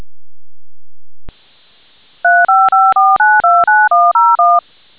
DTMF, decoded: 354929101